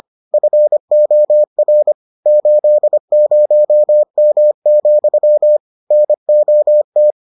Morse code, heard as a keyed tone, about 25 wpm, 600 Hz. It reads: FOR 80M, NOT